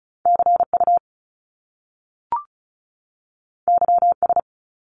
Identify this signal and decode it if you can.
Morse code, heard as a keyed tone, about 36 wpm, 700 Hz. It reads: CU YS
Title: Top Two Letter Combinations in Words 35wpm